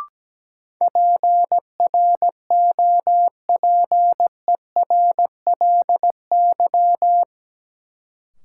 Morse code, heard as a keyed tone, about 17 wpm, 700 Hz. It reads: PROPERLY
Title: Sentences from Top 1100 Words - New Word Drill Alt - 17wpm